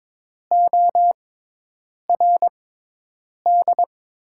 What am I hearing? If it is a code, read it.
Morse code, heard as a keyed tone, about 22 wpm, spaced as for 8 wpm, 700 Hz. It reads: ORD
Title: 6x Character Spacing - Top 300 Words - 22wpm